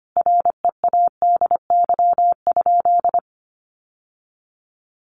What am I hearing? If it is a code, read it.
Morse code, heard as a keyed tone, about 25 wpm, 700 Hz. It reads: READY?
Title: Sentences from Top 1200 Words 25wpm